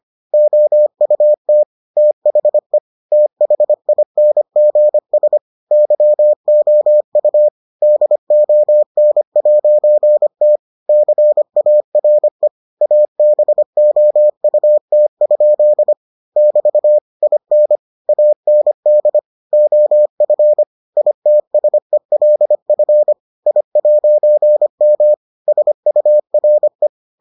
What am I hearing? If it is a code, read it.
Morse code, heard as a keyed tone, about 25 wpm, 600 Hz. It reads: OUT THE THINGS YOU DON'T CARE ABOUT? <BT> IN AND OF ITSELF I'M SURE